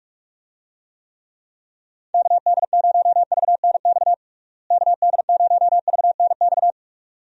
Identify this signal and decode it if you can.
Morse code, heard as a keyed tone, about 41 wpm, 700 Hz. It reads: KD0VNX KD0VNX
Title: USA Call Signs - 6 Characters - Repeated 2x - 45wpm